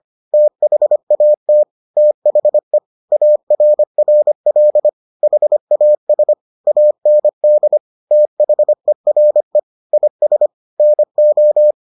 Morse code, heard as a keyed tone, about 25 wpm, 600 Hz. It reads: THAT THE ARRL HAS AND THERE IS NO